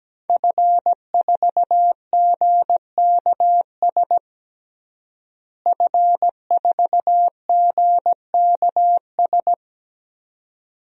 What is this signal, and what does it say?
Morse code, heard as a keyed tone, about 17 wpm, 700 Hz. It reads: F4GKS F4GKS
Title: DXCC-100 Call Signs - 5 Characters - Repeated 2x - 17wpm